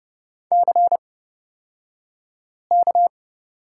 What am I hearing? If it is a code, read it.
Morse code, heard as a keyed tone, about 30 wpm, spaced as for 5 wpm, 700 Hz. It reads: CK
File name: Lesson 37d - Letter x - Two Characters 30wpm